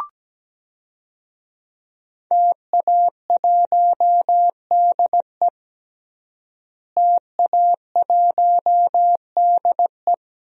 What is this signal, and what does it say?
Morse code, heard as a keyed tone, about 17 wpm, 700 Hz. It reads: TA1DE TA1DE
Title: DXCC-100 Call Signs - 5 Characters - Repeated 2x - 17wpm